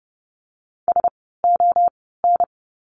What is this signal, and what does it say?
Morse code, heard as a keyed tone, about 30 wpm, spaced as for 17 wpm, 700 Hz. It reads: SON
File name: Lesson 16f - Letter c - Words 30wpm